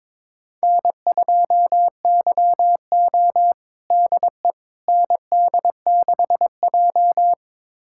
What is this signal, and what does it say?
Morse code, heard as a keyed tone, about 23 wpm, 700 Hz. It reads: N2YO DE ND6J
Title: USA Callsign DE Callsign - 4 Characters - 22wpm